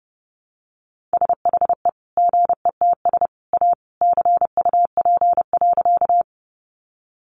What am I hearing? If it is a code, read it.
Morse code, heard as a keyed tone, about 30 wpm, 700 Hz. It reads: SHE GETS A CUP.
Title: Additional Easy Phrases 30wpm